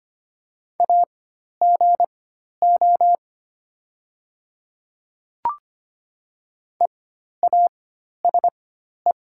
Morse code, heard as a keyed tone, about 25 wpm, spaced as for 12 wpm, 700 Hz. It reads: AGO EASE
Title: 4x Character Spacing - Top 500 Words - 25wpm